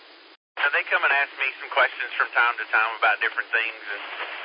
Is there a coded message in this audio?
Yes: Emergency Alert System (SAME) tones from 0.6 to 3.7 s, heard as harsh buzzing bursts, with speech.